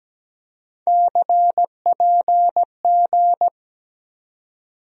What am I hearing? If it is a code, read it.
Morse code, heard as a keyed tone, about 17 wpm, 700 Hz. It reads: CPG
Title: USA Call Sign Postfixes - 3 Letters 17wpm